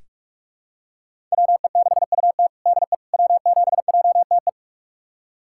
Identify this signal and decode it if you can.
Morse code, heard as a keyed tone, about 46 wpm, 700 Hz. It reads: WE6UT DE W7JTE
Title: USA Callsign DE Callsign - 5 Characters - 45wpm